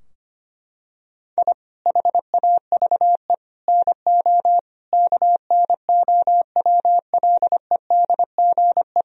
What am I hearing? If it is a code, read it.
Morse code, heard as a keyed tone, about 25 wpm, 700 Hz. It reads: I HAVE NO KNOWLEDGE
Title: Sentences from Top 1000 Words - Review Part 3 - 25wpm